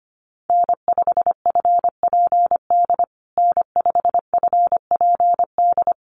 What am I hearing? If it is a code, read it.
Morse code, heard as a keyed tone, about 25 wpm, 700 Hz. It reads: N5FPD N5FPD
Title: USA Call Signs Repeated 3x 25wpm